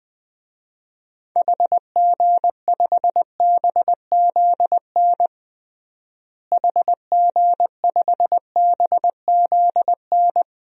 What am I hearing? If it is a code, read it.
Morse code, heard as a keyed tone, about 20 wpm, 700 Hz. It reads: HG5BZN HG5BZN
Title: DXCC-100 Call Signs - 6 Characters - Repeated 3x - 20wpm